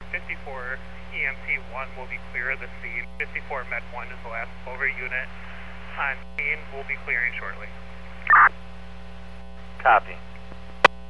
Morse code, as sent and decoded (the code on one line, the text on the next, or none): none